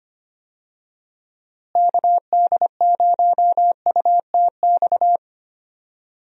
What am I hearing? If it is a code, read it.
Morse code, heard as a keyed tone, about 25 wpm, 700 Hz. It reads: KD0UTX
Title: USA Call Signs - 6 Characters - Repeated 2x - 25wpm